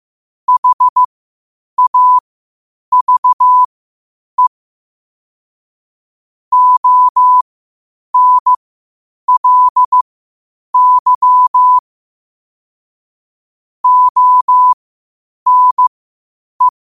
Morse code, heard as a keyed tone, about 15 wpm, spaced as for 8 wpm, 1000 Hz. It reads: HAVE ONLY ONE
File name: qotdfiles / 8wpm